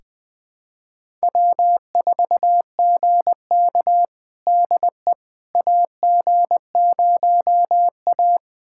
Morse code, heard as a keyed tone, about 20 wpm, 700 Hz. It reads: W4GK DE AG0A
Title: USA Callsign DE Callsign - 4 Characters - 20wpm